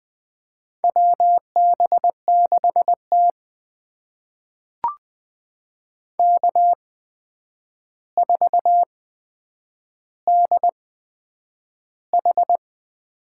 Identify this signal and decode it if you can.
Morse code, heard as a keyed tone, about 20 wpm, spaced as for 9 wpm, 700 Hz. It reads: WB6T K4DH
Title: USA Call Signs - 4 Characters - 8x Character Spacing - 20wpm